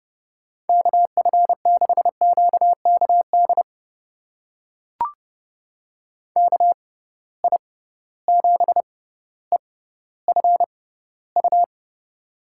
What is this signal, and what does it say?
Morse code, heard as a keyed tone, about 31 wpm, spaced as for 16 wpm, 700 Hz. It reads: KF6QKD KI7EFU
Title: USA Call Signs - 6 Characters - 6x Character Spacing - 30wpm